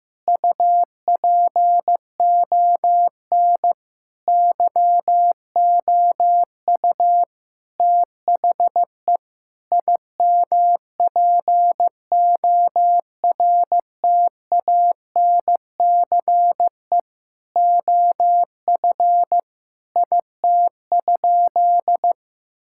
Morse code, heard as a keyed tone, about 15 wpm, 700 Hz. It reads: UPON YOU THE IMPORTANCE OF IT?
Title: Sentences from Top 1600 Words 15wpm